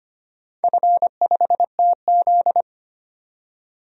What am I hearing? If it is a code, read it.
Morse code, heard as a keyed tone, about 25 wpm, 700 Hz. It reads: F5TZ
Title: DXCC-100 Call Signs - 4 Characters - 25wpm